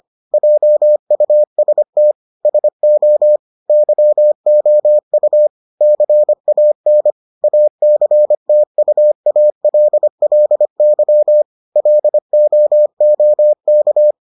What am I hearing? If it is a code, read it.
Morse code, heard as a keyed tone, about 25 wpm, 600 Hz. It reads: JUST SO YOU CAN ACTUALLY LOOK